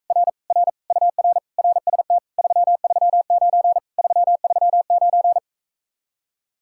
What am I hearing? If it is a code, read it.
Morse code, heard as a keyed tone, about 45 wpm, 700 Hz. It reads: R R UR RST 339 339